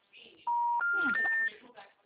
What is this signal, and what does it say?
Special information tone, with speech.